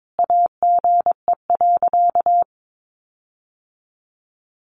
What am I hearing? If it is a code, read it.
Morse code, heard as a keyed tone, about 22 wpm, 700 Hz. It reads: AGE.
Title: Sentences from Top 1300 Words 22wpm